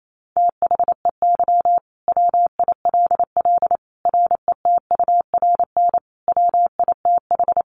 Morse code, heard as a keyed tone, about 28 wpm, 700 Hz. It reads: THEY WILL RETURN WITH
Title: Sentences from Top 800 Words 28wpm